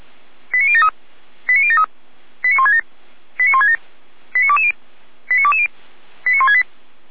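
Five-tone selective call (ZVEI), heard as quick runs of five tones, tones of about 70 ms. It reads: 89073 89073 89167 89167 89220 89220 89168
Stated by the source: ZVEI five-tone selective call received on an SDR, from the Feuerwehr Orlamünde (f85360) fire brigade audio server.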